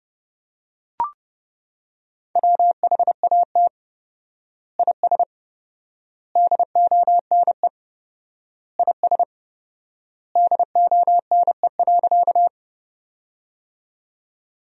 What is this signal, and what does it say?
Morse code, heard as a keyed tone, about 30 wpm, 700 Hz. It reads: WHAT IS DONE IS DONE.
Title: 4x Word-Spacing - Sentences from Top 500 Words - 30wpm